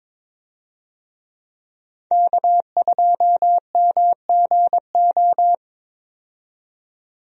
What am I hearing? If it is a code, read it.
Morse code, heard as a keyed tone, about 22 wpm, 700 Hz. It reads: K2MGO
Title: USA Call Signs - 5 Characters - 22wpm